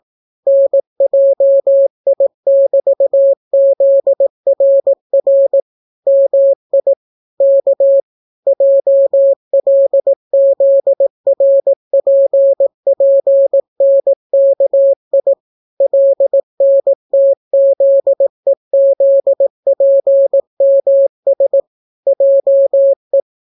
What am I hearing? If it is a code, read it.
Morse code, heard as a keyed tone, about 18 wpm, 550 Hz. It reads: NJI<BT>ZRR MI K JLZRPPNKI LNTZEZPMS JE